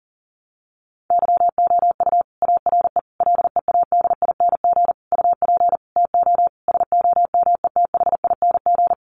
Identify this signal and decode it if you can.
Morse code, heard as a keyed tone, about 40 wpm, 700 Hz. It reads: YOU ARE LEADING UP TO SOMETHING
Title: Sentences from Top 1100 Words 40wpm